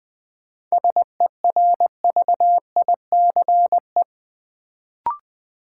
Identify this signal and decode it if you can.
Morse code, heard as a keyed tone, about 20 wpm, 700 Hz. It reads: SERVICE
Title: Sentences from Top 700 Words - New Word Drill - 20wpm